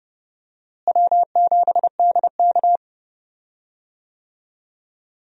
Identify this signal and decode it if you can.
Morse code, heard as a keyed tone, about 30 wpm, 700 Hz. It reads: W7DK
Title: USA Call Signs - 4 Characters - Repeated 2x - 30wpm